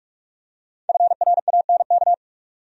Morse code, heard as a keyed tone, about 46 wpm, 700 Hz. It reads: FRANK
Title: USA Common Names - 45wpm